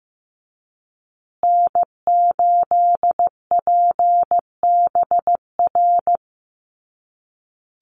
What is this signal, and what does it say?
Morse code, heard as a keyed tone, about 15 wpm, 700 Hz. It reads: N8PBR
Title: USA Call Signs Repeated 2x 15wpm